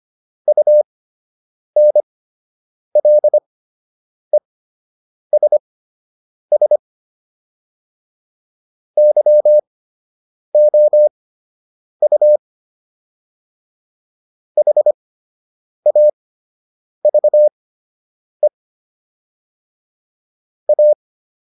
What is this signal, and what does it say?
Morse code, heard as a keyed tone, about 25 wpm, spaced as for 8 wpm, 600 Hz. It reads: UNLESS YOU HAVE A